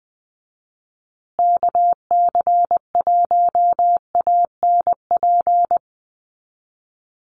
Morse code, heard as a keyed tone, about 20 wpm, 700 Hz. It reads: KC1ANP
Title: USA Call Signs Repeated 2x 20wpm